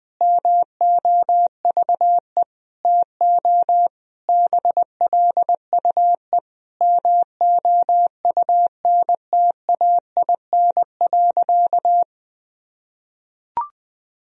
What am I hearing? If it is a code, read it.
Morse code, heard as a keyed tone, about 20 wpm, 700 Hz. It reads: MOVE TO BLUE MOUNTAIN.
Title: Sentences from Top 1000 Words - Review Part 3 - 20wpm